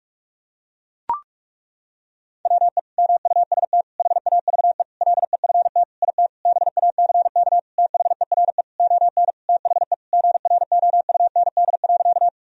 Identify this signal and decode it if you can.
Morse code, heard as a keyed tone, about 41 wpm, 700 Hz. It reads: WE MUST HAVE LEFT IT BACK THERE ON THE GROUND.
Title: Sentences from Top 1000 Words - Review Part 2 - 45wpm